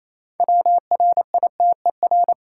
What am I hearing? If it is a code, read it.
Morse code, heard as a keyed tone, about 28 wpm, 700 Hz. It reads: WRITER
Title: Six Letter Words - 28wpm